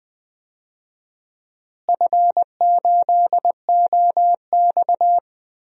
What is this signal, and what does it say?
Morse code, heard as a keyed tone, about 20 wpm, 700 Hz. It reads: F8OX
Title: DXCC-100 Call Signs - 4 Characters - Repeated 2x - 20wpm